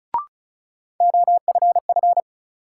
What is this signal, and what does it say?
Morse code, heard as a keyed tone, about 36 wpm, 700 Hz. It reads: OFF
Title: Sets of 2 Words - New Word Drill - 35wpm